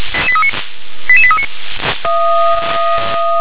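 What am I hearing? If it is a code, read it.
Five-tone selective call (ZVEI), heard as quick runs of five tones, tones of about 70 ms. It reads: 88839
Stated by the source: ZVEI five-tone alert received by SDR.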